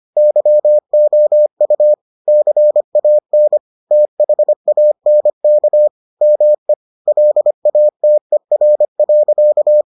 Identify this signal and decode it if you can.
Morse code, heard as a keyed tone, about 25 wpm, 600 Hz. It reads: YOU CAN THANK ME LATER.